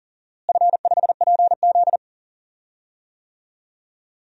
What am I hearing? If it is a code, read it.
Morse code, heard as a keyed tone, about 39 wpm, 700 Hz. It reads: F5PZ
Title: DXCC-100 Call Signs - 4 Characters - Repeated 3x - 40wpm